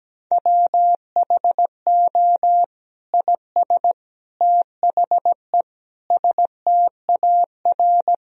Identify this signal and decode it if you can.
Morse code, heard as a keyed tone, about 17 wpm, 700 Hz. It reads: WHO IS THE STAR
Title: Sentences from Top 1000 Words - Review Part 2 - 17wpm